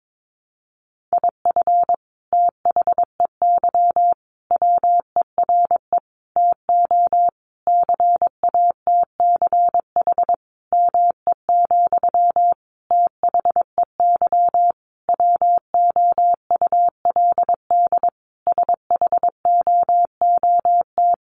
Morse code, heard as a keyed tone, about 22 wpm, 700 Hz. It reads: IF THEY WERE TO CATCH ME, THEY WOULD SHOOT
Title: Sentences from Top 1500 Words 22wpm